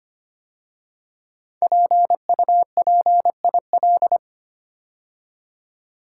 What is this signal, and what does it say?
Morse code, heard as a keyed tone, about 25 wpm, 700 Hz. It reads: PUPIL
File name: 5 Letter Words - First 3 Letters Given - 25wpm